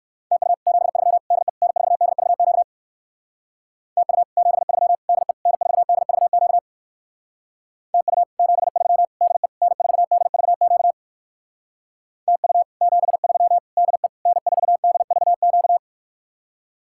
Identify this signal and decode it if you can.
Morse code, heard as a keyed tone, about 59 wpm, 700 Hz. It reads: TU 73 DE N4DVQ TU 73 DE N4DVQ TU 73 DE N4DVQ TU 73 DE N4DVQ